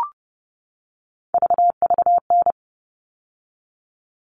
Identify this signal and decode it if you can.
Morse code, heard as a keyed tone, about 30 wpm, 700 Hz. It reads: VVN
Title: All Cut Numbers - 3-Digit 30wpm